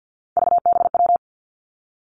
Morse code, heard as a keyed tone, about 50 wpm, 700 Hz. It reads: VBW
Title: Three Challenging Letter-Number Combinations 50wpm